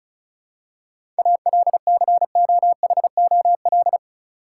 Morse code, heard as a keyed tone, about 36 wpm, 700 Hz. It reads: ALCOHOL
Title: Seven Letter Words - 35wpm